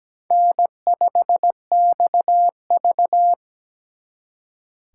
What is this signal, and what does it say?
Morse code, heard as a keyed tone, about 20 wpm, 700 Hz. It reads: N5XV
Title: Speed Racing - USA Call Signs 17wpm